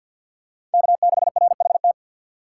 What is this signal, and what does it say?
Morse code, heard as a keyed tone, about 47 wpm, 700 Hz. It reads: K6RHT
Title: USA Call Signs - 5 Characters - 3x Character Spacing - 50wpm